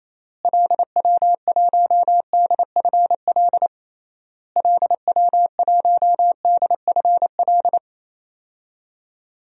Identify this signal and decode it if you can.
Morse code, heard as a keyed tone, about 28 wpm, 700 Hz. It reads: LW1DFL LW1DFL
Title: DXCC-100 Call Signs - 6 Characters - Repeated 3x - 28wpm